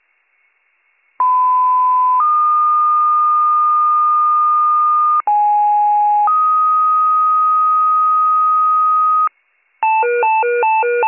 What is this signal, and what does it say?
Two-tone page, heard as a one-second tone, then a three-second tone, about 1010 then 1220 Hz, about 830 then 1250 Hz. It is heard from 1.2 to 9.3 s.